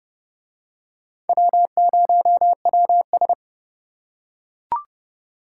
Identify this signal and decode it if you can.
Morse code, heard as a keyed tone, about 29 wpm, 700 Hz. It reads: W0WS